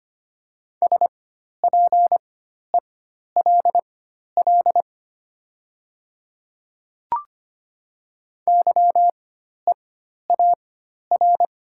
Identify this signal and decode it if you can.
Morse code, heard as a keyed tone, about 25 wpm, spaced as for 12 wpm, 700 Hz. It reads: SPELL YEAR
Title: 4x Character Spacing - Top 300 Words - 25wpm